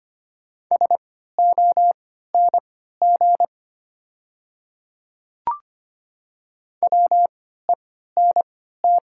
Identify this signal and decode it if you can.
Morse code, heard as a keyed tone, about 25 wpm, spaced as for 14 wpm, 690 Hz. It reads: SONG WENT
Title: 3x Character Spacing - Top 500 Words - 25wpm